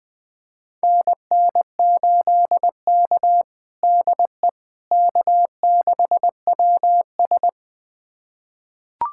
Morse code, heard as a keyed tone, about 21 wpm, 700 Hz. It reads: NN8K DE K6WS
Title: USA Callsign DE Callsign - 4 Characters - 20wpm